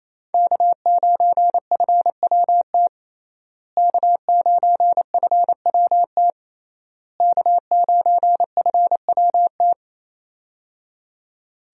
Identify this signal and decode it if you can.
Morse code, heard as a keyed tone, about 28 wpm, 700 Hz. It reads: K9FWT K9FWT K9FWT
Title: USA Call Signs - 5 Characters - Repeated 3x - 28wpm